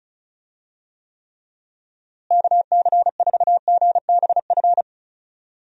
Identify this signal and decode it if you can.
Morse code, heard as a keyed tone, about 36 wpm, 700 Hz. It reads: KC4GBF
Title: USA Call Signs - 6 Characters - Repeated 3x - 35wpm